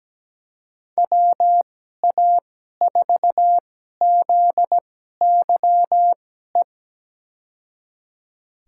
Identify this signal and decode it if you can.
Morse code, heard as a keyed tone, about 17 wpm, spaced as for 12 wpm, 700 Hz. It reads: WA4ZYE
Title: USA Call Signs - 6 Characters - 2x Character Spacing - 17wpm